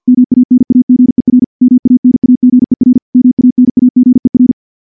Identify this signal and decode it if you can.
Morse code, heard as a keyed tone, about 51 wpm, 250 Hz. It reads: MANAGER MANAGER MANAGER
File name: morse / morse-training-50WPM